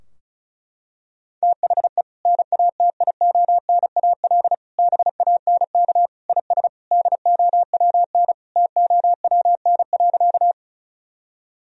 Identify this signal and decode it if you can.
Morse code, heard as a keyed tone, about 36 wpm, 700 Hz. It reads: THE NATIONAL BANK IS DOWN TOWN.
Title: Sentences from Top 1000 Words - Review Part 4 - 35wpm